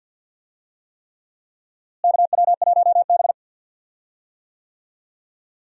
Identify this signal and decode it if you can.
Morse code, heard as a keyed tone, about 52 wpm, 700 Hz. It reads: KW1B
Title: USA Call Signs - 4 Characters - Repeated 3x - 50wpm